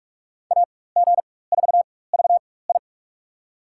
Morse code, heard as a keyed tone, about 46 wpm, spaced as for 21 wpm, 700 Hz. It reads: AG4VI
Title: USA Call Signs - 5 Characters - 4x Character Spacing - 45wpm